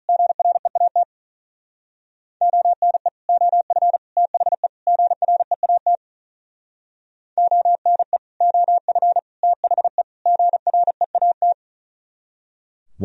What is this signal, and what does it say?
Morse code, heard as a keyed tone, about 43 wpm, 700 Hz. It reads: GREAT ONE OF THE GREAT ONE OF THE GREAT
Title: Speed-Racing - Sets of 4 Words 35wpm